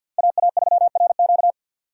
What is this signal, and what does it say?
Morse code, heard as a keyed tone, about 51 wpm, 700 Hz. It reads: AA3RQ
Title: USA Call Signs - 5 Characters - 4x Character Spacing - 50wpm